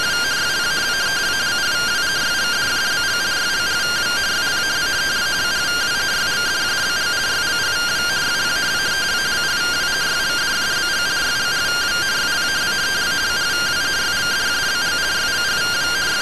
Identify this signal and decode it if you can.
Two-tone teleprinter signal, about 170 Hz shift.